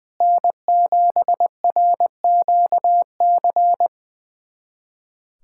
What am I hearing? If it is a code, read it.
Morse code, heard as a keyed tone, about 20 wpm, 700 Hz. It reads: N7RQC
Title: USA Callsign DE Callsign - 5 Characters - 20wpm